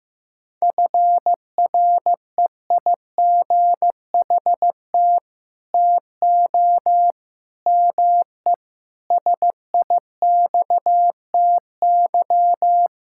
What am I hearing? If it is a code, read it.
Morse code, heard as a keyed tone, about 15 wpm, 700 Hz. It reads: FREIGHT TO ME SIXTY
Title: Pangram Challenge v2 - 15wpm